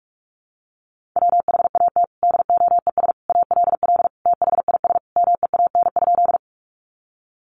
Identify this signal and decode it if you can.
Morse code, heard as a keyed tone, about 44 wpm, 700 Hz. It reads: WHAT DOES ALL THIS MEAN?
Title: Sentences from Top 200 Words 45wpm